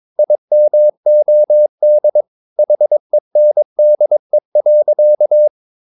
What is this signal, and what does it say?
Morse code, heard as a keyed tone, about 22 wpm, 600 Hz. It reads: IMOD HENDE.
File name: Ferie QRS 22wpm